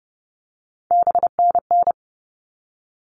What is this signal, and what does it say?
Morse code, heard as a keyed tone, about 30 wpm, 700 Hz. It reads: BNN
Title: All Cut Numbers - 3-Digit 30wpm